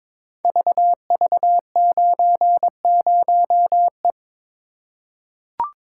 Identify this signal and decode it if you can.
Morse code, heard as a keyed tone, about 22 wpm, 700 Hz. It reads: VV90E
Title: Code Groups - 5 Letter:Numbers - 22wpm